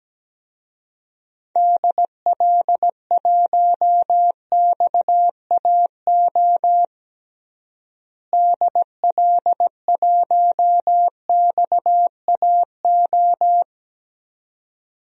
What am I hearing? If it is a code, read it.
Morse code, heard as a keyed tone, about 17 wpm, 700 Hz. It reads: DL1XAO DL1XAO
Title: DXCC-100 Call Signs - 6 Characters - Repeated 2x - 17wpm